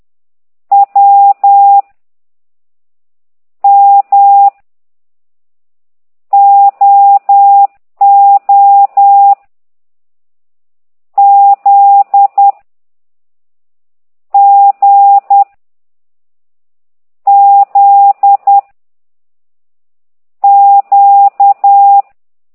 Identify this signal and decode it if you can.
Morse code, heard as a keyed tone, about 10 wpm, 800 Hz. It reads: W M O O Z G Z Q